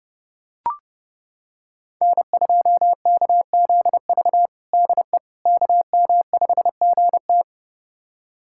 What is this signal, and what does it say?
Morse code, heard as a keyed tone, about 30 wpm, 700 Hz. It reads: N2KZV DE KM5GT
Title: USA Callsign DE Callsign - 5 Characters - 30wpm